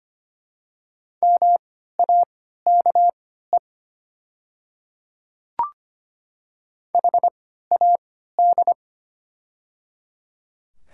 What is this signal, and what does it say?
Morse code, heard as a keyed tone, about 25 wpm, spaced as for 14 wpm, 700 Hz. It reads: MAKE HAD
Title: 3x Character Spacing - Top 300 Words - 25wpm